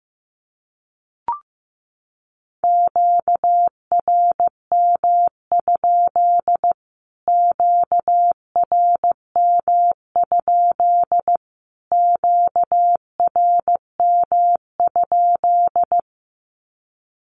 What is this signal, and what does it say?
Morse code, heard as a keyed tone, about 15 wpm, 700 Hz. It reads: QRM? QRM? QRM?
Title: Q-Codes Repeated 3x 15wpm